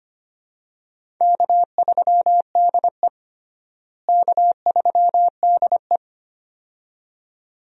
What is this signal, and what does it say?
Morse code, heard as a keyed tone, about 25 wpm, 700 Hz. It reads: K3DE K3DE
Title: USA Call Signs - 4 Characters - Repeated 2x - 25wpm